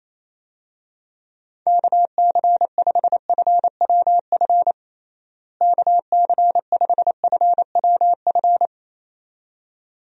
Morse code, heard as a keyed tone, about 28 wpm, 700 Hz. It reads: KC5FWF KC5FWF
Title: USA Call Signs - 6 Characters - Repeated 2x - 28wpm